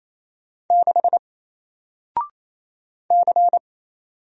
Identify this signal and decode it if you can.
Morse code, heard as a keyed tone, about 27 wpm, 700 Hz. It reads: 6 C
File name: Single-Letter-Number-28wpm